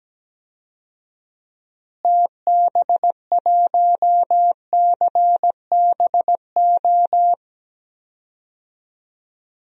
Morse code, heard as a keyed tone, about 17 wpm, 700 Hz. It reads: TB1CBO
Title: DXCC-100 Call Signs - 6 Characters - Repeated 3x - 17wpm